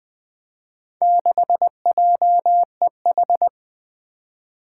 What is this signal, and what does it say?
Morse code, heard as a keyed tone, about 20 wpm, 700 Hz. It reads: 6JEH
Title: Four Challenging Letter-Number Combinations 20wpm